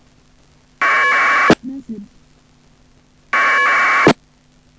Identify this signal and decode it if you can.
Packet radio (AX.25, as in APRS), with speech.
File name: uploader / iss aprs